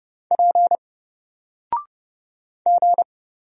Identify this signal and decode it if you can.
Morse code, heard as a keyed tone, about 30 wpm, 700 Hz. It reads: P G
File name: Single-Letter-Number-30wpm